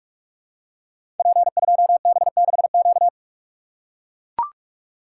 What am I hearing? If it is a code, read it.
Morse code, heard as a keyed tone, about 44 wpm, 700 Hz. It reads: P2B6Q
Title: Code Groups - 5 Letter:Numbers - 45wpm